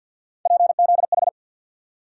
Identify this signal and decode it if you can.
Morse code, heard as a keyed tone, about 49 wpm, 700 Hz. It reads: PZH
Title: USA Call Sign Postfixes - 3 Letters 50wpm